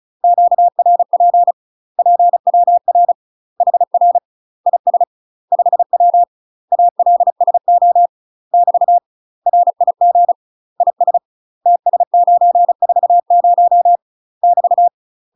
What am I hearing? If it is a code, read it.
Morse code, heard as a keyed tone, about 36 wpm, 700 Hz. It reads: QRP PWR HR IS 5W ALSO <BT> RIG IS TS940 <BT>
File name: UK MASTER QSOs TRAINING / 35WPM / UK MASTER QSOs 0026